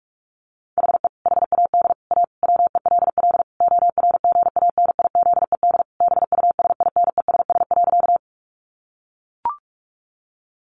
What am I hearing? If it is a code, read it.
Morse code, heard as a keyed tone, about 44 wpm, 700 Hz. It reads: HE HAD A WELL ORGANIZED BUSINESS.
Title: Sentences from Top 1300 Words 45wpm